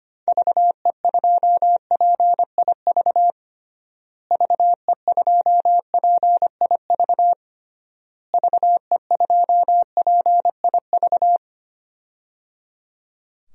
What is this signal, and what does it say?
Morse code, heard as a keyed tone, about 25 wpm, 700 Hz. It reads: VE2PIV VE2PIV VE2PIV
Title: DXCC-100 Call Signs - 6 Characters - Repeated 3x - 25wpm